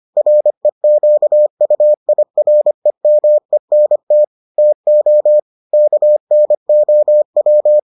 Morse code, heard as a keyed tone, about 25 wpm, 600 Hz. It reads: REQUIREMENT TO KNOW